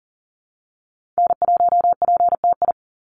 Morse code, heard as a keyed tone, about 40 wpm, 700 Hz. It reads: N1PTI
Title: USA Call Signs 40wpm